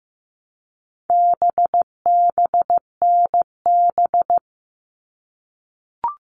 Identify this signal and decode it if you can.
Morse code, heard as a keyed tone, about 15 wpm, 700 Hz. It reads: BBNB
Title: All Cut Numbers - 4-Digit 15wpm